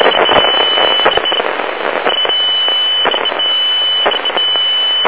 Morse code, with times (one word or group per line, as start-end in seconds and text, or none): none